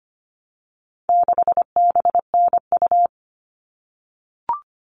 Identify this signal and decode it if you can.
Morse code, heard as a keyed tone, about 25 wpm, 700 Hz. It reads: 6BNU
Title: All Cut Numbers - 4-Digit 25wpm